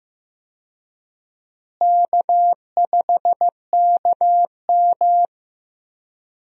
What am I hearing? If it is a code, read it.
Morse code, heard as a keyed tone, about 16 wpm, 700 Hz. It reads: K5KM
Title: USA Call Signs - 4 Characters - Repeated 3x - 15wpm